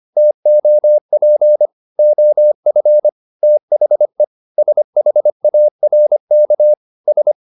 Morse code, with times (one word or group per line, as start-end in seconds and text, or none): 0.2-1.7 TOP
2.0-3.1 OF
3.4-4.2 THE
4.6-6.7 SHARK
7.1-7.3 S